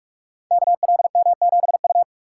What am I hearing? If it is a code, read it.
Morse code, heard as a keyed tone, about 44 wpm, 700 Hz. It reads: KLM7U
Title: Code Groups - 5 Letter:Numbers - 45wpm